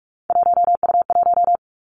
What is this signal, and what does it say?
Morse code, heard as a keyed tone, about 46 wpm, 700 Hz. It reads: 1U1
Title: Three Challenging Letter-Number Combinations 45wpm